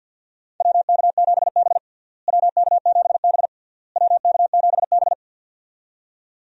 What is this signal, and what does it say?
Morse code, heard as a keyed tone, about 53 wpm, 700 Hz. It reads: WK7B WK7B WK7B
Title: USA Call Signs - 4 Characters - Repeated 3x - 50wpm